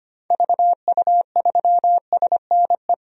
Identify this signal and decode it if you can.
Morse code, heard as a keyed tone, about 25 wpm, 700 Hz. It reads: VU3SNE
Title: DXCC-100 Call Signs - 6 Characters - 25wpm